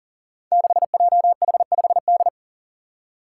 Morse code, heard as a keyed tone, about 37 wpm, 700 Hz. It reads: 6JH5D
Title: Five Challenging Letter-Number Combinations 40wpm